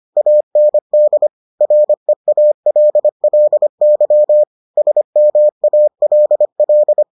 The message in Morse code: AND REALLY SMALL

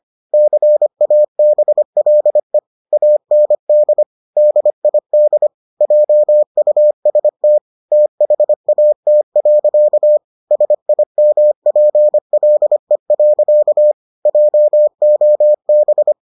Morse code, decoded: CABLE AND DID JUST THAT. SIMPLE. JOB

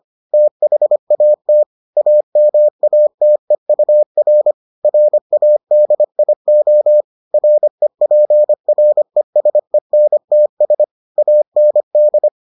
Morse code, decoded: THAT AMATEUR RADIO REPRESENTS AND

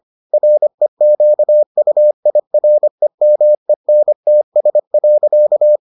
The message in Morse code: REQUIREMENTS.

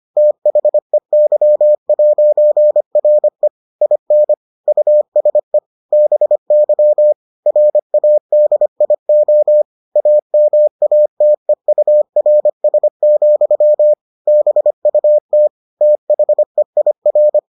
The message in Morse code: THEY'RE IN USE BY RADIO AMATEURS, BUT THEIR

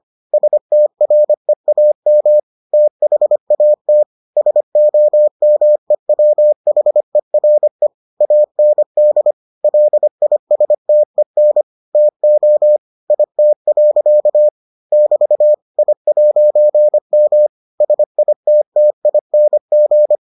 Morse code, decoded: STREAM THAT SOMEWHERE AND LISTEN TO IT. <BT> I'M SITTING

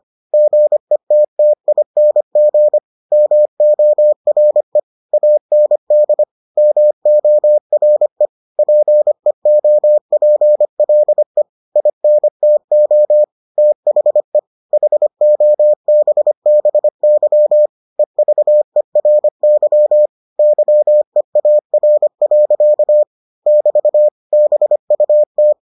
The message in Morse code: GETTING MORE AND MORE PEOPLE INTO THE HOBBY EVERY YEAR. <BT> BUT

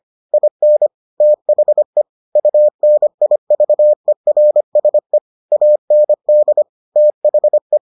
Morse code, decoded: IN THE UNIVERSE AND THE

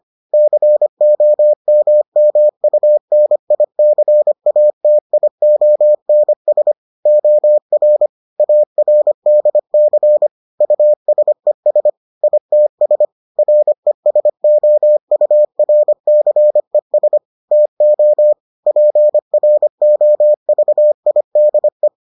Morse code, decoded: COMMUNICATIONS OR ARDC USES ITS RESOURCES TO PROVIDE

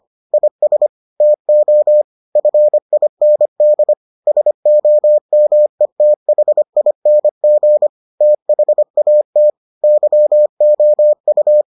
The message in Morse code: IS TO FIND SOMETHING THAT YOU

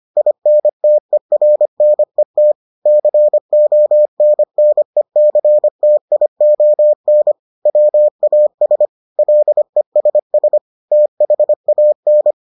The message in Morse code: INTERNET CONNECTION WAS LESS THAN

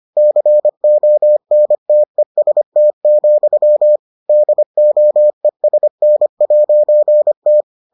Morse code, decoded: CONTEST, DOESN'T

